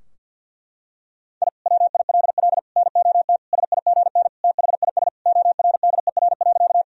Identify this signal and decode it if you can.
Morse code, heard as a keyed tone, about 52 wpm, 700 Hz. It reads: I WILL NOT SIGN THIS ORDER.
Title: Sentences from Top 1000 Words - Review Part 2 - 50wpm